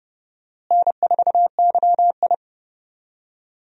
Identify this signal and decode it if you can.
Morse code, heard as a keyed tone, about 30 wpm, 700 Hz. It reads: N4YI
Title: USA Call Signs - 4 Characters - Repeated 2x - 30wpm